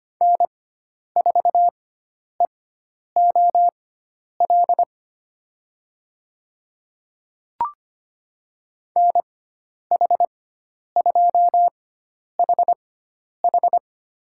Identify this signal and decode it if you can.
Morse code, heard as a keyed tone, about 25 wpm, spaced as for 10 wpm, 700 Hz. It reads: N4EOL NH2HH
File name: Lesson 16g - Letter c - Call Signs 25wpm